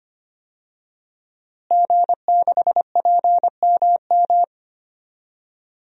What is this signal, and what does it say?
Morse code, heard as a keyed tone, about 25 wpm, 700 Hz. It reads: G6PMM
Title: DXCC-100 Call Signs - 5 Characters - Repeated 2x - 25wpm